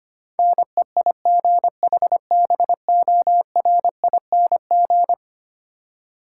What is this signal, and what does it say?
Morse code, heard as a keyed tone, about 25 wpm, 700 Hz. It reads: NEIGHBORING
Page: morse-code-ninja-practice-25wpm